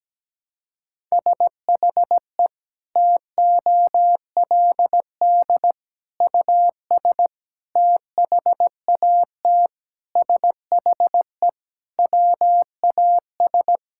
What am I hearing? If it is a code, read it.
Morse code, heard as a keyed tone, about 17 wpm, 700 Hz. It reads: SHE TOLD US THAT SHE WAS
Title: Sentences from Top 1000 Words - Review Part 6 - 17wpm